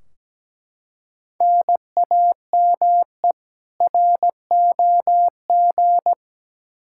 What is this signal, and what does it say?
Morse code, heard as a keyed tone, about 17 wpm, 700 Hz. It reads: NAME ROG
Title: Practice QSO - Exchange 1 - Beginning - 17wpm